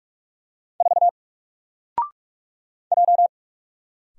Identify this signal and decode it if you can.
Morse code, heard as a keyed tone, about 44 wpm, 700 Hz. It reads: 4 J
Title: Single Letter-Number Alt 45wpm